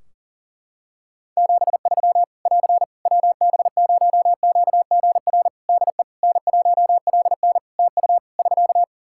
Morse code, heard as a keyed tone, about 40 wpm, 700 Hz. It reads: 73 <AR> WB0QGR DE N1LN TU <SK>
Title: Practice QSO - Exchange 4 - Ending - 40wpm